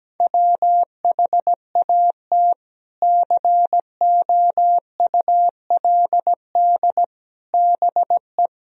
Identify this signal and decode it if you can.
Morse code, heard as a keyed tone, about 17 wpm, 700 Hz. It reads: WHAT COULD BE
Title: Sentences from Top 400 Words and Abbreviations 17wpm